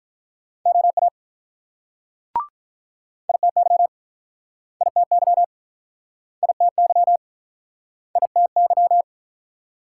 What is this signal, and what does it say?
Morse code, heard as a keyed tone, about 47 wpm, 690 Hz. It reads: OA ITY ITY ITY ITY